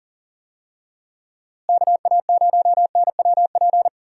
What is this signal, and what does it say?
Morse code, heard as a keyed tone, about 40 wpm, 700 Hz. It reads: KA0NWP
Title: USA Call Signs - 6 Characters - Repeated 2x - 40wpm